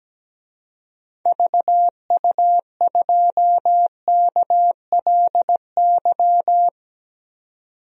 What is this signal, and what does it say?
Morse code, heard as a keyed tone, about 17 wpm, 700 Hz. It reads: VU2KLY
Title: DXCC-100 Call Signs - 6 Characters - 17wpm